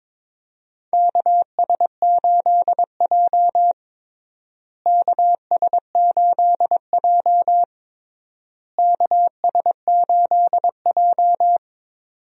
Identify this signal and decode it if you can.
Morse code, heard as a keyed tone, about 22 wpm, 700 Hz. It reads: KS8J KS8J KS8J
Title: USA Call Signs - 4 Characters - Repeated 3x - 22wpm